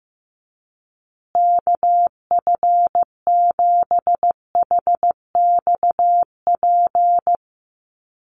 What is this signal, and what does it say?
Morse code, heard as a keyed tone, about 15 wpm, 700 Hz. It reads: KF7HXP
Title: USA Call Signs Repeated 3x 15wpm